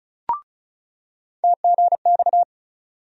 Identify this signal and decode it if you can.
Morse code, heard as a keyed tone, about 34 wpm, 700 Hz. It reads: TGX
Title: USA Call Sign Postfixes - 3 Letters 35wpm